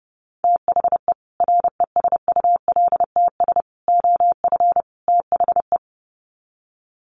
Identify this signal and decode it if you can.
Morse code, heard as a keyed tone, about 30 wpm, 700 Hz. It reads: THE RESULTS OF THE
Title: Sets of 4 Words 30wpm